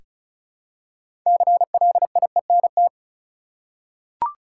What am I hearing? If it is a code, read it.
Morse code, heard as a keyed tone, about 34 wpm, 700 Hz. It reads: CLIENT
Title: Six Letter Words - 35wpm